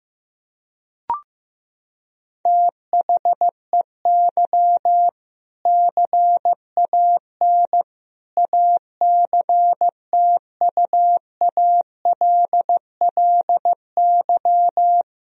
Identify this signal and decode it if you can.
Morse code, heard as a keyed tone, about 15 wpm, 700 Hz. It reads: THEY CAN ACTUALLY
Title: Sentences from Top 1000 Words - Review Part 4 - 15wpm